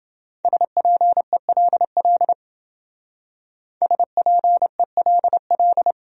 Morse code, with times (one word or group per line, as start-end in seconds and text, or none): 0.5-2.3 SPELL
3.8-5.9 SPELL